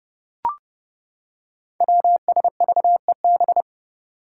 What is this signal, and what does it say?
Morse code, heard as a keyed tone, about 30 wpm, 700 Hz. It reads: WSVEB
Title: Five Challenging Letter-Number Combinations 30wpm